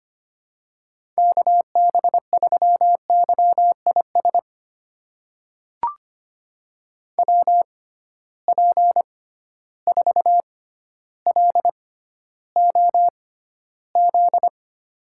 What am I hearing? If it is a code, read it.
Morse code, heard as a keyed tone, about 25 wpm, spaced as for 13 wpm, 700 Hz. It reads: KB3YIS WP4LOZ
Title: USA Call Signs - 6 Characters - 6x Character Spacing - 25wpm